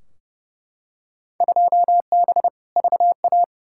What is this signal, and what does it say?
Morse code, heard as a keyed tone, about 32 wpm, 700 Hz. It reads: 2B VA
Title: Field Day Exchanges Encore - 30wpm